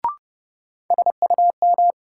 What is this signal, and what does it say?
Morse code, heard as a keyed tone, about 30 wpm, 700 Hz. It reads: SUM
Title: Sentences from Top 1000 Words - Review Part 6 - New Word Drill - 30wpm